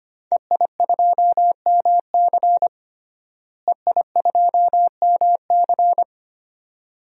Morse code, heard as a keyed tone, about 25 wpm, 700 Hz. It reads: EI2MC EI2MC
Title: DXCC-100 Call Signs - 5 Characters - Repeated 2x - 25wpm